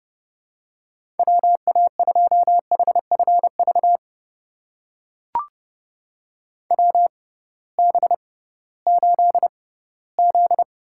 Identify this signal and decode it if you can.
Morse code, heard as a keyed tone, about 30 wpm, spaced as for 18 wpm, 700 Hz. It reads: WA2HFV WB8Z
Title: USA Call Signs - 6 Characters - 6x Character Spacing - 30wpm